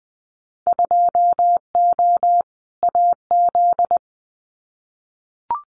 Morse code, heard as a keyed tone, about 20 wpm, 700 Hz. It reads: 2O AZ
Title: Winter Field Day Exchanges 20wpm